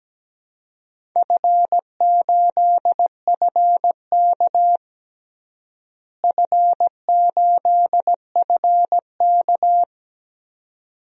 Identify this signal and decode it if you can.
Morse code, heard as a keyed tone, about 17 wpm, 700 Hz. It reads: F8FK F8FK
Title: DXCC-100 Call Signs - 4 Characters - Repeated 2x - 17wpm